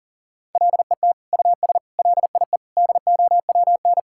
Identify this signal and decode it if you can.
Morse code, heard as a keyed tone, about 40 wpm, 700 Hz. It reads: LET US LIE DOWN
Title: Sentences from Top 1600 Words 40wpm